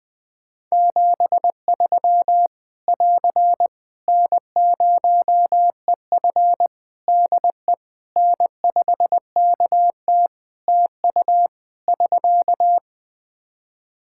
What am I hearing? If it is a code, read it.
Morse code, heard as a keyed tone, about 20 wpm, 700 Hz. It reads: 73 <AR> N0EF DE N5KT TU <SK>
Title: Practice QSO - Exchange 4 - Ending - 20wpm